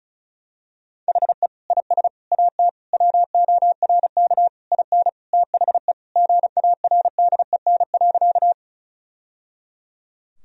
Morse code, heard as a keyed tone, about 34 wpm, 700 Hz. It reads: HE IS AT WORK IN THE GARDEN.
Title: Sentences from Top 1000 Words - Review Part 4 - 35wpm